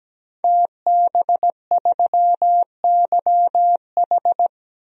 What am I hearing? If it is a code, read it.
Morse code, heard as a keyed tone, about 17 wpm, 700 Hz. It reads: TB3YH
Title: DXCC-100 Call Signs - 5 Characters - 17wpm